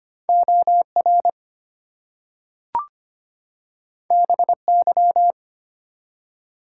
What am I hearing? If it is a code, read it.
Morse code, heard as a keyed tone, about 25 wpm, 700 Hz. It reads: OR BY
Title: Two Letter Words - 25wpm